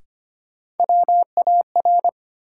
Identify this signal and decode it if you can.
Morse code, heard as a keyed tone, about 25 wpm, 700 Hz. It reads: WAR
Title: Sets of 5 Words - New Word Drill - 25wpm